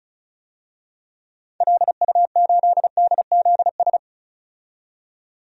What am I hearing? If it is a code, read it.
Morse code, heard as a keyed tone, about 36 wpm, 700 Hz. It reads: LU8DZS
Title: DXCC-100 Call Signs - 6 Characters - 35wpm